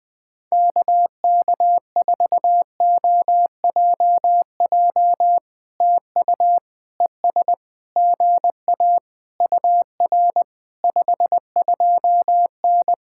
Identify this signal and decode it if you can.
Morse code, heard as a keyed tone, about 20 wpm, 700 Hz. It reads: KK4OJJ TU ES GA UR 52N
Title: POTA QSO Practice - 20wpm